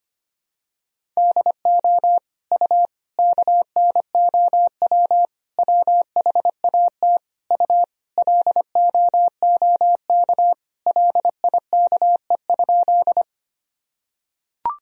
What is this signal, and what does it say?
Morse code, heard as a keyed tone, about 25 wpm, 700 Hz. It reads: DO U KNOW WHAT U LOOK LIKE?
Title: Sentences from Top 100 Words and Abbreviations 25wpm